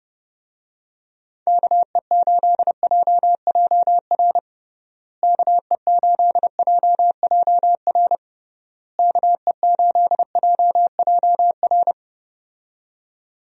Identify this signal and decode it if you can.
Morse code, heard as a keyed tone, about 30 wpm, 700 Hz. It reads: KE8JJR KE8JJR KE8JJR
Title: USA Call Signs - 6 Characters - Repeated 3x - 30wpm